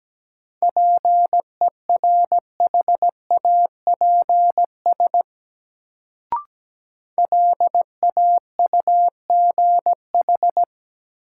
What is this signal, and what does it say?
Morse code, heard as a keyed tone, about 17 wpm, 700 Hz. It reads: PERHAPS LAUGH
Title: Sentences from Top 500 Words - New Word Drill - 17wpm